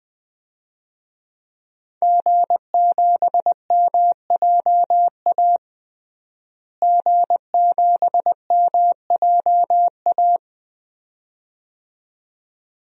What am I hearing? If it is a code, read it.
Morse code, heard as a keyed tone, about 21 wpm, 700 Hz. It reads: G7MJA G7MJA
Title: DXCC-100 Call Signs - 5 Characters - Repeated 2x - 20wpm